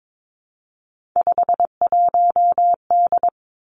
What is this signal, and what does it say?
Morse code, heard as a keyed tone, about 22 wpm, 700 Hz. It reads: 51D
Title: Three Challenging Letter-Number Combinations 22wpm